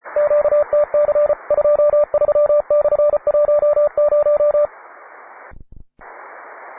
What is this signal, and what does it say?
Morse code, heard as a keyed tone, about 34 wpm, 590 Hz, with radio noise. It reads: QTC 23/10
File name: wae / cw advanced / py2na cw